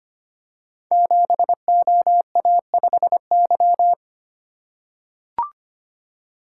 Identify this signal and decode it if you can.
Morse code, heard as a keyed tone, about 25 wpm, 700 Hz. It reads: 7OA5Y
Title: Code Groups - 5 Letter:Numbers - 25wpm